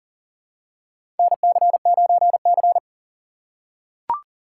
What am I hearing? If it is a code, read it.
Morse code, heard as a keyed tone, about 40 wpm, 700 Hz. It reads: NC9C
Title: USA Call Signs - 4x Character Spacing - 40wpm